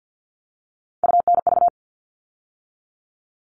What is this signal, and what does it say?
Morse code, heard as a keyed tone, about 50 wpm, 700 Hz. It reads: UNV
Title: All Cut Numbers - 3-Digit 50wpm